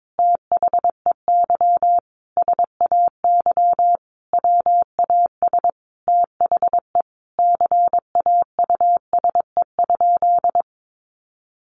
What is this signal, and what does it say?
Morse code, heard as a keyed tone, about 22 wpm, 700 Hz. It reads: THEY SAY WAS THE CAUSE?
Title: Sentences from Top 200 Words 22wpm